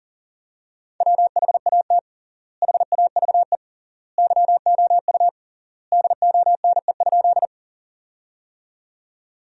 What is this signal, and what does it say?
Morse code, heard as a keyed tone, about 40 wpm, 700 Hz. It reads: WHAT HAVE YOU DONE?
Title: 3x Word-Spacing - Sentences from Top 500 Words - 40wpm